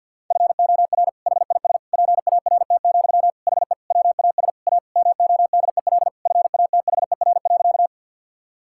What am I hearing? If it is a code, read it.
Morse code, heard as a keyed tone, about 51 wpm, 690 Hz. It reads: FOR HIS PART, HE WAS A MODEL FATHER.
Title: Sentences from Top 1000 Words - Review Part 5 - 50wpm